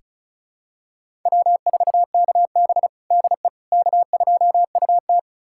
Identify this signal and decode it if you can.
Morse code, heard as a keyed tone, about 34 wpm, 700 Hz. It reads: W4KB DE K2UT
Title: USA Callsign DE Callsign - 4 Characters - 35wpm